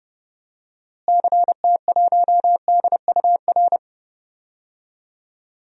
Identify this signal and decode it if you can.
Morse code, heard as a keyed tone, about 30 wpm, 700 Hz. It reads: CT1DUR
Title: DXCC-100 Call Signs - 6 Characters - Repeated 3x - 30wpm